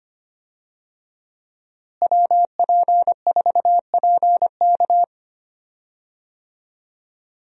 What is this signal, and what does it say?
Morse code, heard as a keyed tone, about 25 wpm, 700 Hz. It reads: WP4PK
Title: USA Call Signs - 5 Characters - Repeated 2x - 25wpm